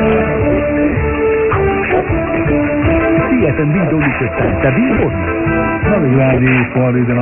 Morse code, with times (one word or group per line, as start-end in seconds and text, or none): none